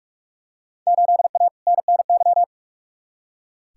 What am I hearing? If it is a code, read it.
Morse code, heard as a keyed tone, about 42 wpm, 700 Hz. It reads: 8A NNY
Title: Field Day Exchanges Encore - 45wpm